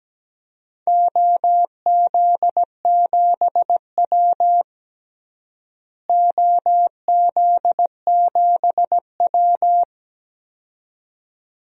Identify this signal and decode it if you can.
Morse code, heard as a keyed tone, about 17 wpm, 700 Hz. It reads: OZ7W OZ7W
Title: DXCC-100 Call Signs - 4 Characters - Repeated 2x - 17wpm